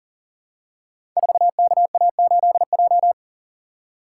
Morse code, heard as a keyed tone, about 40 wpm, 700 Hz. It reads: 4KA8J
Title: Code Groups - 5 Letter:Numbers - 40wpm